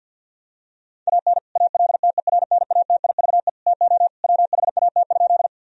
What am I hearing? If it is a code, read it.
Morse code, heard as a keyed tone, about 50 wpm, 700 Hz. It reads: AN ALTERNATIVE TO WHAT?
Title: Sentences from Top 1000 Words - Review Part 4 - 50wpm